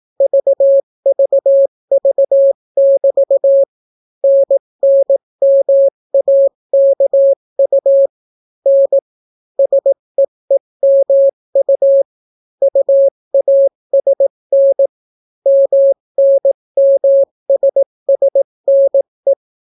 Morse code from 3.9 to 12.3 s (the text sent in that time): NNMAKU N SEEMU